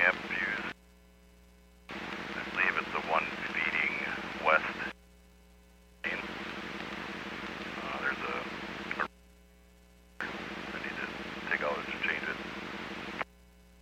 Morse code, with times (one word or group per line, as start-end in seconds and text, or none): none